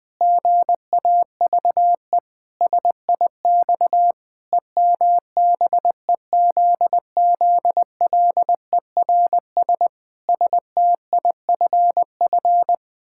Morse code, heard as a keyed tone, about 20 wpm, 700 Hz. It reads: GAVE SIX EMBEZZLERS STIFF
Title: Pangram Challenge v2 - 20wpm